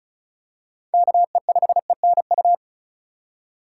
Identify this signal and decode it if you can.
Morse code, heard as a keyed tone, about 36 wpm, 700 Hz. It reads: KE5ENU
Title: USA Call Signs - 6 Characters - 4x Character Spacing - 35wpm